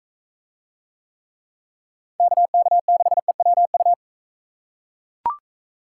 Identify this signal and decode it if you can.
Morse code, heard as a keyed tone, about 43 wpm, 700 Hz. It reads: KK6EWU